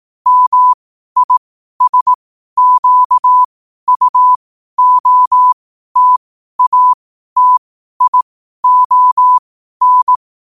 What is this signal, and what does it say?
Morse code, heard as a keyed tone, about 18 wpm, spaced as for 12 wpm, 1000 Hz. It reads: MISQUOTATION
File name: qotdfiles / 12wpm